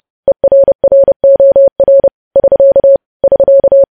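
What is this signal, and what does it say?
Morse code, heard as a keyed tone, about 30 wpm, 550 Hz. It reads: ERROR <SK> <SK>
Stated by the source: News headlines sent in Morse code at 30 wpm.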